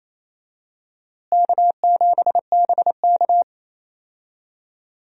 Morse code, heard as a keyed tone, about 28 wpm, 700 Hz. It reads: K7BK
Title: USA Call Signs - 4 Characters - Repeated 3x - 28wpm